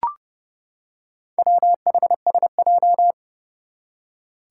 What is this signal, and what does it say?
Morse code, heard as a keyed tone, about 30 wpm, 700 Hz. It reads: WHSJ
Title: Four Challenging Letter-Number Combinations 30wpm